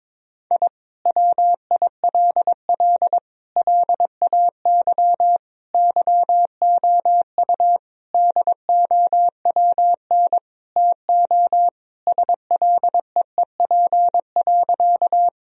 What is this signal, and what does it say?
Morse code, heard as a keyed tone, about 22 wpm, 700 Hz. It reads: I WILL LAY YOU DOWN TO SLEEP.
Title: Sentences from Top 1000 Words - Review Part 2 - 22wpm